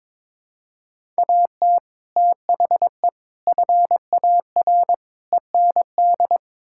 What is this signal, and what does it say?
Morse code, heard as a keyed tone, about 22 wpm, 700 Hz. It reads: AT THE FAR END
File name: practice / 22wpm / Build A Phrase - 5-Word Phrases - 22wpm